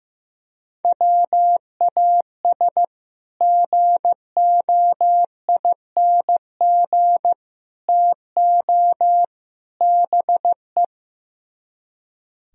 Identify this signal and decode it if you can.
Morse code, heard as a keyed tone, about 15 wpm, 700 Hz. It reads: WAS GOING TO BE
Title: Speed-Racing - Sets of 4 Words 15wpm